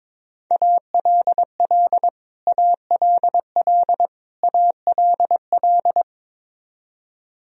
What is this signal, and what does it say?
Morse code, heard as a keyed tone, about 22 wpm, 700 Hz. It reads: ALL ALL ALL
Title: Top 100 Words Repeated 22wpm